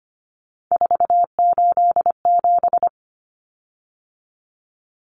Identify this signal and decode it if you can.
Morse code, heard as a keyed tone, about 25 wpm, 700 Hz. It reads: 487
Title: Common Cut Numbers - 3-Digit 25wpm